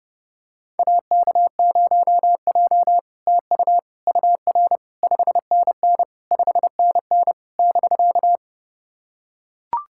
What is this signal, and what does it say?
Morse code, heard as a keyed tone, about 30 wpm, 700 Hz. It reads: AK0J TU UR 5NN 5NN <BK>
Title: SOTA QSO Practice 30wpm